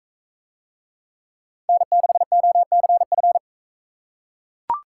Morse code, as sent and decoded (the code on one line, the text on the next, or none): -. -.... --- -.-. ..-.
N6OCF